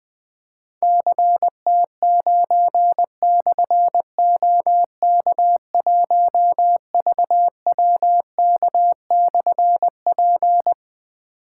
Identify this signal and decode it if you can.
Morse code, heard as a keyed tone, about 20 wpm, 700 Hz. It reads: CT9/OK1VWK/P
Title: Difficult Call Signs 20wpm